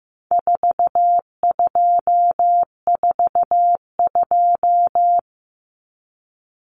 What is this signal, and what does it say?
Morse code, heard as a keyed tone, about 15 wpm, 700 Hz. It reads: 4242
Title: Numbers - 4-Digit 15wpm